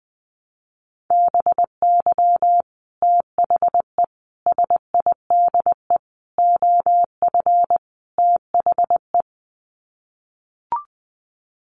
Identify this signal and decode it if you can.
Morse code, heard as a keyed tone, about 20 wpm, 700 Hz. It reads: BY THE SIDE OF THE
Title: Sets of 5 Words Encore2 20wpm